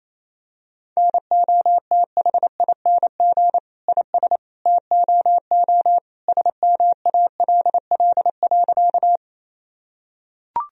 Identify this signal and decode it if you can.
Morse code, heard as a keyed tone, about 27 wpm, 700 Hz. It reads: NOTHING IS TOO SMALL.
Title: Sentences from Top 1000 Words - Review Part 1 - 28wpm